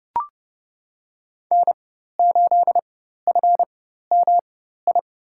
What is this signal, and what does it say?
Morse code, heard as a keyed tone, about 31 wpm, spaced as for 14 wpm, 700 Hz. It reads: N8FMI
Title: USA Call Signs - 5 Characters - 4x Character Spacing - 30wpm